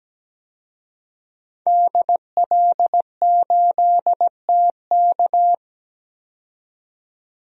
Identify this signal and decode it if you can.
Morse code, heard as a keyed tone, about 17 wpm, 700 Hz. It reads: DL8TK
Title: DXCC-100 Call Signs - 5 Characters - 17wpm